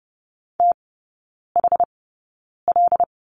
Morse code, heard as a keyed tone, about 30 wpm, spaced as for 9 wpm, 700 Hz. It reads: THL
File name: Lesson 17e - Letter u - Three Characters 30wpm